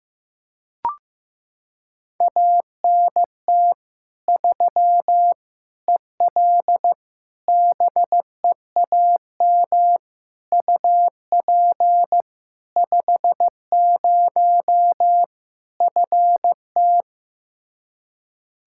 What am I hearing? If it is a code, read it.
Morse code, heard as a keyed tone, about 15 wpm, 700 Hz. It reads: ANT 3 EL BEAM UP 50 FT
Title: Practice QSO - Exchange 2 - Info - 15wpm